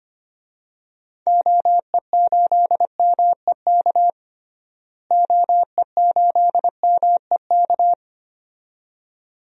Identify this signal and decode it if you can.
Morse code, heard as a keyed tone, about 25 wpm, 700 Hz. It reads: OE8MEK OE8MEK
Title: DXCC-100 Call Signs - 6 Characters - Repeated 2x - 25wpm